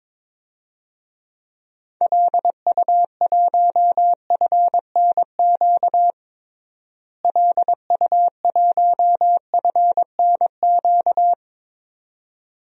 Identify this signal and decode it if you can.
Morse code, heard as a keyed tone, about 21 wpm, 700 Hz. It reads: LU1FNQ LU1FNQ
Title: DXCC-100 Call Signs - 6 Characters - Repeated 2x - 22wpm